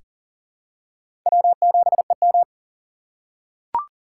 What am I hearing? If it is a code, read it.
Morse code, heard as a keyed tone, about 40 wpm, 700 Hz. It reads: W7EM
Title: USA Call Signs - 4 Characters - 6x Character Spacing - 40wpm